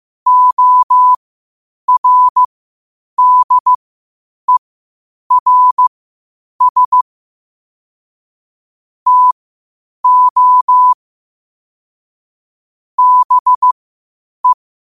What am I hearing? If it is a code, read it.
Morse code, heard as a keyed tone, about 15 wpm, spaced as for 8 wpm, 1000 Hz. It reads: ORDERS TO BE
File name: qotdfiles / 8wpm